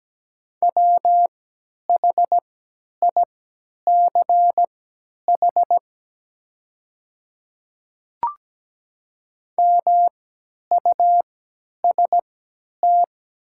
Morse code, heard as a keyed tone, about 17 wpm, spaced as for 10 wpm, 700 Hz. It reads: WHICH MUST
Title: 3x Character Spacing - Top 300 Words - 17wpm